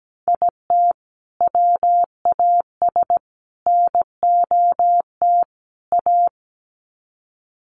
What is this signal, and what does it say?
Morse code, heard as a keyed tone, about 17 wpm, 700 Hz. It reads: IT WAS NOT A
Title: Sets of 4 Words 17wpm